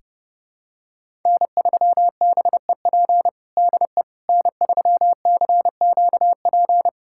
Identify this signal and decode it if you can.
Morse code, heard as a keyed tone, about 30 wpm, 700 Hz. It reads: N3BEP DE N3CQP
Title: USA Callsign DE Callsign - 5 Characters - 30wpm